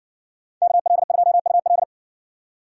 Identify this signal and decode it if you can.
Morse code, heard as a keyed tone, about 60 wpm, 700 Hz. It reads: KL2UL